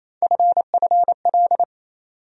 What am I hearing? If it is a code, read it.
Morse code, heard as a keyed tone, about 28 wpm, 700 Hz. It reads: FFL
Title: USA Call Sign Postfixes - 3 Letters 28wpm